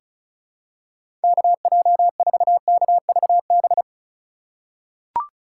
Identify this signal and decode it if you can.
Morse code, heard as a keyed tone, about 36 wpm, 700 Hz. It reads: KJ4KVB
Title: USA Call Signs - 6 Characters - 2x Character Spacing - 35wpm